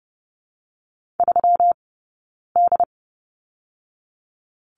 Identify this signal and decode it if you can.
Morse code, heard as a keyed tone, about 30 wpm, 700 Hz. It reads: 3 D
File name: Lesson 20d - Number 3 - Two Characters 30wpm